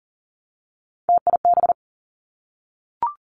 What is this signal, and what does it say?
Morse code, heard as a keyed tone, about 40 wpm, 700 Hz. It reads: TIB
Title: Three Challenging Letter-Number Combinations 40wpm